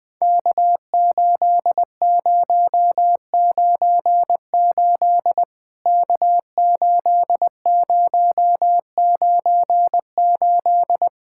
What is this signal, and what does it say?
Morse code, heard as a keyed tone, about 20 wpm, 700 Hz. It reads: K8098 K8098
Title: POTA QSO Practice - 20wpm